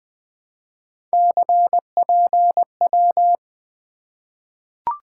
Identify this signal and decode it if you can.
Morse code, heard as a keyed tone, about 20 wpm, 700 Hz. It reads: CPW
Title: Three Letter-Number Combinations - 20wpm